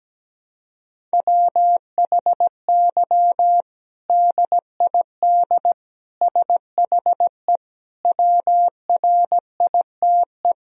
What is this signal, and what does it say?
Morse code, heard as a keyed tone, about 17 wpm, 700 Hz. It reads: WHY DID SHE WRITE
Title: Sentences from Top 1000 Words - Review Part 1 - 17wpm